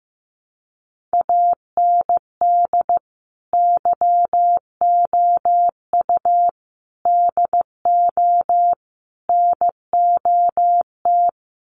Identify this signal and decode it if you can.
Morse code, heard as a keyed tone, about 15 wpm, 700 Hz. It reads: AND YOU DO NOT
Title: Sets of 4 Words 15wpm